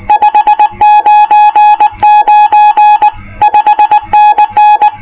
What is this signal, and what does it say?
Morse code, heard as a keyed tone, about 19 wpm, 840 Hz. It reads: 599 5NN